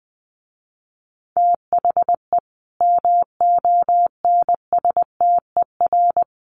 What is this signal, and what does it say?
Morse code, heard as a keyed tone, about 20 wpm, 700 Hz. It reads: THE MONSTER
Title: A Sound Of Thunder 20wpm